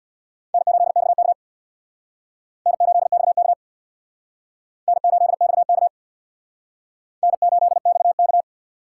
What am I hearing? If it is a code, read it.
Morse code, heard as a keyed tone, about 60 wpm, 700 Hz. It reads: N8XK N8XK N8XK N8XK
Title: Speed Racing - USA 1x2 Call Signs 50wpm